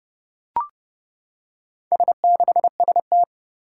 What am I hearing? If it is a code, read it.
Morse code, heard as a keyed tone, about 30 wpm, 700 Hz. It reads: S6ST
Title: Four Challenging Letter-Number Combinations 30wpm